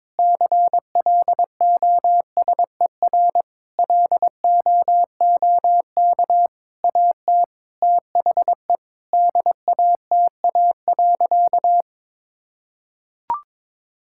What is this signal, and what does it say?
Morse code, heard as a keyed tone, about 22 wpm, 700 Hz. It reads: CLOSER LOOK AT THE DATA.
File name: Sentences with 7-Word Phrases - 22wpm